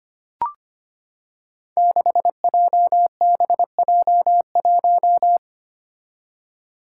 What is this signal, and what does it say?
Morse code, heard as a keyed tone, about 25 wpm, 700 Hz. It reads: 6JBJ1
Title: Five Challenging Letter-Number Combinations 25wpm